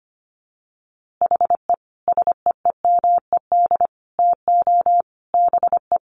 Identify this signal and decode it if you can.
Morse code, heard as a keyed tone, about 25 wpm, 700 Hz. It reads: HE SEEMED TO BE
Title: A Sound Of Thunder 25wpm